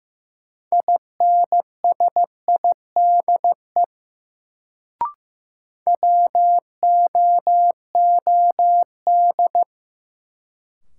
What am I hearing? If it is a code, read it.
Morse code, heard as a keyed tone, about 15 wpm, 700 Hz. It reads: INSIDE WOOD
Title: Sentences from Top 1000 Words - Review Part 2 - New Word Drill - 15wpm